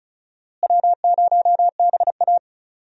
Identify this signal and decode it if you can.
Morse code, heard as a keyed tone, about 36 wpm, 700 Hz. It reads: W0BA
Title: USA Call Signs - 4 Characters - Repeated 2x - 35wpm